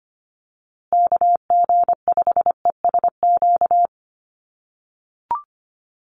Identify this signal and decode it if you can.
Morse code, heard as a keyed tone, about 25 wpm, 700 Hz. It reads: KG5ESQ
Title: USA Call Signs Repeated 3x 25wpm